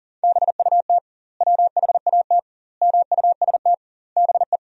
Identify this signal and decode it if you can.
Morse code, heard as a keyed tone, about 40 wpm, 700 Hz. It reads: BUT WHAT MUST BE
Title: 2x Word-Spacing - Sentences from Top 500 Words - 40wpm